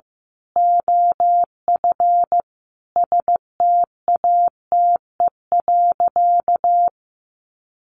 Morse code, heard as a keyed tone, about 15 wpm, 700 Hz. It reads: OF STATE.
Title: Sentences from Top 1400 Words 15wpm